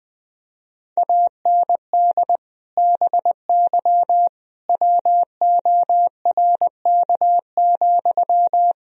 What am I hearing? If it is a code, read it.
Morse code, heard as a keyed tone, about 20 wpm, 700 Hz. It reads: AND BY WORK,
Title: Sentences from Top 1000 Words - Review Part 4 - 20wpm